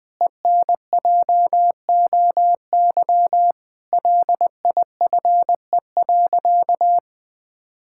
Morse code, heard as a keyed tone, about 19 wpm, 700 Hz. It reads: ENJOY LIFE.
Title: Sentences with 8-Word Phrases - 20wpm